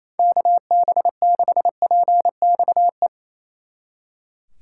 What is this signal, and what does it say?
Morse code, heard as a keyed tone, about 34 wpm, 700 Hz. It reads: KB6PXE
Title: Speed Racing - USA Call Signs 28wpm